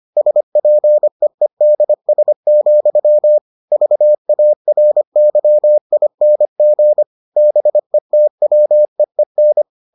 Morse code, heard as a keyed tone, about 25 wpm, 600 Hz. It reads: SPEEDS, VARYING BETWEEN